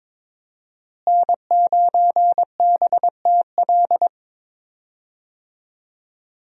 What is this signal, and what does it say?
Morse code, heard as a keyed tone, about 22 wpm, 700 Hz. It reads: N9BTL
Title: USA Call Signs - 5 Characters - Repeated 3x - 22wpm